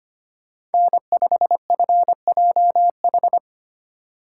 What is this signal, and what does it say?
Morse code, heard as a keyed tone, about 31 wpm, 700 Hz. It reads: N5FJH
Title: Speed Racing - USA 1x3 Call Signs 25wpm